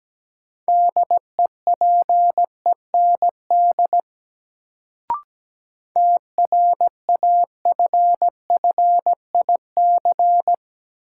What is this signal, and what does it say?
Morse code, heard as a keyed tone, about 17 wpm, 700 Hz. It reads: DEPEND TRAFFIC
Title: Sentences from Top 1300 Words - New Word Drill - 17wpm